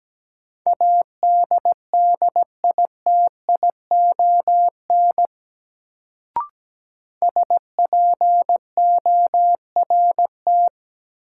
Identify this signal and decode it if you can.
Morse code, heard as a keyed tone, about 17 wpm, 700 Hz. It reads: ADDITION SPORT
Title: Sentences from Top 1500 Words - New Word Drill Alt - 17wpm